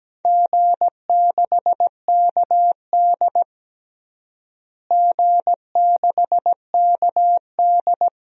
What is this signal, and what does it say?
Morse code, heard as a keyed tone, about 17 wpm, 700 Hz. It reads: G6KD G6KD
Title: DXCC-100 Call Signs - 4 Characters - Repeated 3x - 17wpm